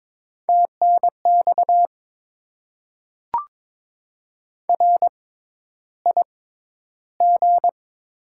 Morse code, heard as a keyed tone, about 22 wpm, spaced as for 11 wpm, 700 Hz. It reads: TNX RIG
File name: Top 20 QSO Elements - 6x Character Spacing - 22wpm